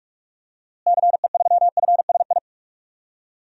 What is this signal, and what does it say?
Morse code, heard as a keyed tone, about 46 wpm, 700 Hz. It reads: CE3FSI
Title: DXCC-100 Call Signs - 6 Characters - Repeated 2x - 45wpm